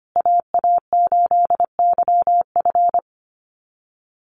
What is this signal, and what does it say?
Morse code, heard as a keyed tone, about 25 wpm, 700 Hz. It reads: AA8YF
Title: USA Call Signs Repeated 3x 25wpm